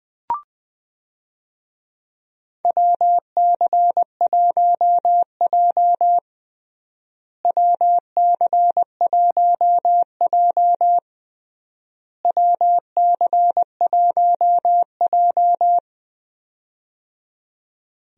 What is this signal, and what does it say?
Morse code, heard as a keyed tone, about 20 wpm, 690 Hz. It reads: WC1J WC1J WC1J
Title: USA Call Signs - 4 Characters - Repeated 3x - 20wpm